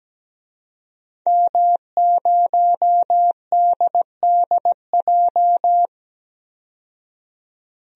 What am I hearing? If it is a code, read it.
Morse code, heard as a keyed tone, about 17 wpm, 700 Hz. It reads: M0DDJ
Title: DXCC-100 Call Signs - 5 Characters - 17wpm